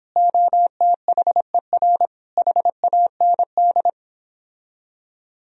Morse code, heard as a keyed tone, about 29 wpm, 700 Hz. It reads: OTHER HAND